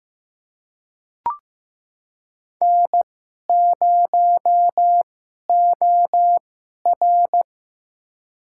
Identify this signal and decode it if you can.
Morse code, heard as a keyed tone, about 15 wpm, spaced as for 11 wpm, 700 Hz. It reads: N0OR
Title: USA Call Signs - 2x Character Spacing - 15wpm